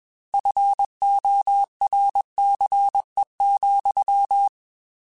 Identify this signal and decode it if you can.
Morse code, heard as a keyed tone, about 21 wpm, 800 Hz. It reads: FORCE,
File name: Technical Subjects / Technical Subjects-mp3s / XL @20